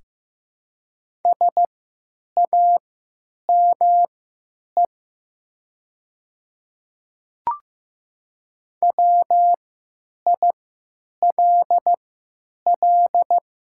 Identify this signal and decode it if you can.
Morse code, heard as a keyed tone, about 15 wpm, spaced as for 9 wpm, 700 Hz. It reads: SAME WILL
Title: 3x Character Spacing - Top 300 Words - 15wpm